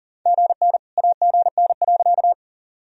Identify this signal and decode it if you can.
Morse code, heard as a keyed tone, about 40 wpm, 700 Hz. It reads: GN AGN.
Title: Sentences from Top 300 Words and Abbreviations 40wpm